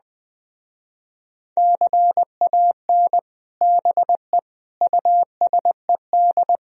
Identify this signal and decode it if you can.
Morse code, heard as a keyed tone, about 20 wpm, 700 Hz. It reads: CAN BE USED
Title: Speed-Racing - Sets of 4 Words 20wpm